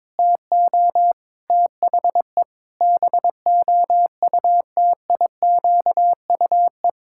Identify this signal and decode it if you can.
Morse code, heard as a keyed tone, about 22 wpm, 700 Hz. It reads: TO THE BOUTIQUE
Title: Pangram Challenge v2 - 22wpm